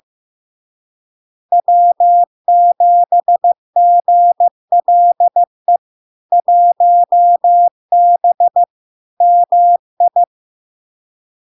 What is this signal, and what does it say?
Morse code, heard as a keyed tone, about 15 wpm, 700 Hz. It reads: W7GLE 1B MI
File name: LARK-FD-2022-Log-in-CW-15wpm-700-Hz